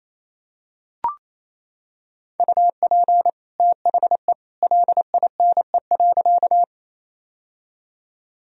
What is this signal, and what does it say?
Morse code, heard as a keyed tone, about 28 wpm, 700 Hz. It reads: UP THE LINE.
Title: Sentences from Top 1000 Words - Review Part 1 - 28wpm